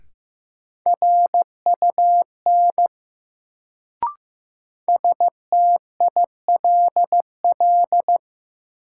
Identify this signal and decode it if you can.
Morse code, heard as a keyed tone, about 15 wpm, 700 Hz. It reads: RUN STILL
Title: Sentences from Top 300 Words - New Word Drill - 15wpm